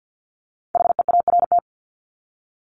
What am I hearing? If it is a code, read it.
Morse code, heard as a keyed tone, about 50 wpm, 700 Hz. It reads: HEART
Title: Top 1001-1250 Words 50wpm